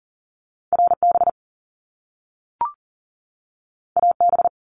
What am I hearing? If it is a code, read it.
Morse code, heard as a keyed tone, about 41 wpm, 700 Hz. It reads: RB AB
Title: Top Two Letter Combinations in Words 40wpm